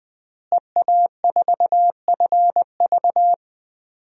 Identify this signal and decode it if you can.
Morse code, heard as a keyed tone, about 20 wpm, 700 Hz. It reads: EA4FV
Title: DXCC-100 Call Signs - 5 Characters - Repeated 2x - 20wpm